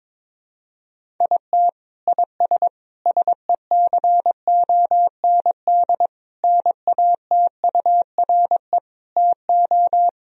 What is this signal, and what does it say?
Morse code, heard as a keyed tone, about 21 wpm, 700 Hz. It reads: IT IS SECOND NATURE TO
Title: Sentences from Top 1000 Words - Review Part 1 - 22wpm